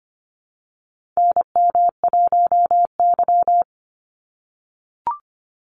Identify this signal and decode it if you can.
Morse code, heard as a keyed tone, about 25 wpm, 700 Hz. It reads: NM1Y
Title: Call Signs - 25wpm